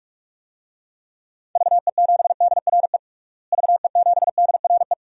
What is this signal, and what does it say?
Morse code, heard as a keyed tone, about 46 wpm, 700 Hz. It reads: VE7DRE VE7DRE
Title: DXCC-100 Call Signs - 6 Characters - Repeated 2x - 45wpm